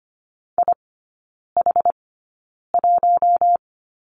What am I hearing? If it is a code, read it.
Morse code, heard as a keyed tone, about 25 wpm, spaced as for 9 wpm, 690 Hz. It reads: IH1
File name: Lesson 14e - Number 2 - Three Characters 25wpm